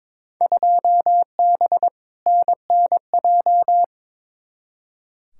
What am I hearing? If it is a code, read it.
Morse code, heard as a keyed tone, about 22 wpm, 700 Hz. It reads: 2B NNJ
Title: Field Day Exchanges Encore - 22wpm